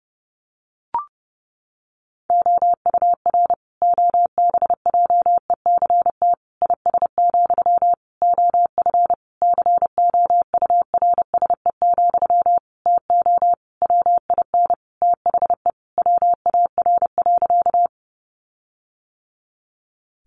Morse code, heard as a keyed tone, about 30 wpm, 700 Hz. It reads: OUR OBJECT IS, OF COURSE, TO WIN THE WAR.
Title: Sentences from Top 1500 Words 30wpm